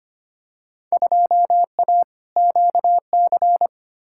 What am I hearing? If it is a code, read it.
Morse code, heard as a keyed tone, about 25 wpm, 700 Hz. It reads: 2A QC
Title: Field Day Exchanges - 25wpm